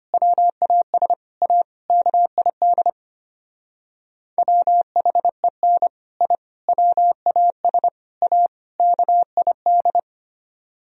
Morse code, heard as a keyed tone, about 31 wpm, 700 Hz. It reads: WAS A KID WHEN I WAS A KID
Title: Speed-Racing - Sets of 5 Words 25wpm